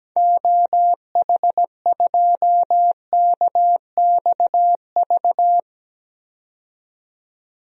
Morse code, heard as a keyed tone, about 17 wpm, 700 Hz. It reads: OH2KXV
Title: DXCC-100 Call Signs - 6 Characters - 17wpm